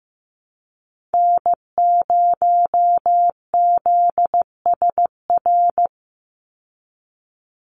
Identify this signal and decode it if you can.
Morse code, heard as a keyed tone, about 15 wpm, 700 Hz. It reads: N0ZSR
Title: USA Call Signs 15wpm